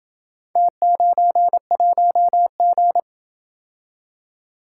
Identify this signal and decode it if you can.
Morse code, heard as a keyed tone, about 26 wpm, 700 Hz. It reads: T91G